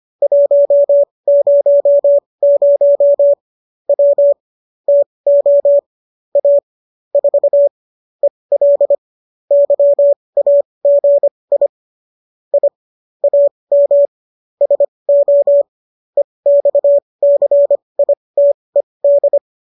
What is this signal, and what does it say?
Morse code, heard as a keyed tone, about 25 wpm, spaced as for 20 wpm, 570 Hz. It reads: 100 W TO A 4 EL YAGI I AM SO EXCITED